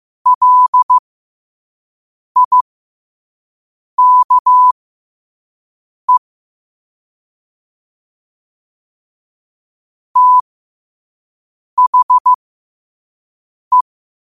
Morse code, heard as a keyed tone, about 15 wpm, spaced as for 5 wpm, 1000 Hz. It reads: LIKE THE